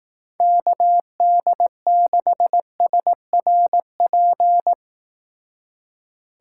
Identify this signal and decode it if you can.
Morse code, heard as a keyed tone, about 18 wpm, 700 Hz. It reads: KD6SRP
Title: Speed Racing - USA 2x3 Call Signs 15wpm